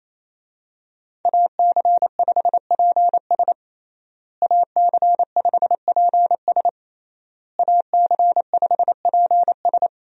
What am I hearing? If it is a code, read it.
Morse code, heard as a keyed tone, about 28 wpm, 700 Hz. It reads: AC5PS AC5PS AC5PS
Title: USA Call Signs - 5 Characters - Repeated 3x - 28wpm